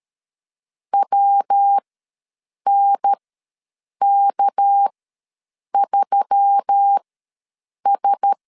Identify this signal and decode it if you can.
Morse code, heard as a keyed tone, about 13 wpm, spaced as for 7 wpm, 800 Hz. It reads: WNK3S
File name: LEZIONI / 8 YJ45 / 8 YJ45 ws 12wpm 3146